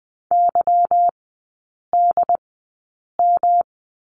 Morse code, heard as a keyed tone, about 20 wpm, spaced as for 8 wpm, 700 Hz. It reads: YDM
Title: 152 Lesson 25e - Letter p - Three Characters 20wpm